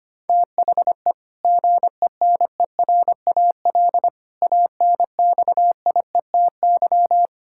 Morse code, heard as a keyed tone, about 25 wpm, 700 Hz. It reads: THE GENERAL ANXIETY
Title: Sentences from Top 1600 Words 25wpm